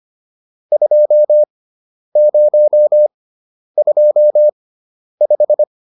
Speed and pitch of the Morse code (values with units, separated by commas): 25 wpm, spaced as for 10 wpm, 600 Hz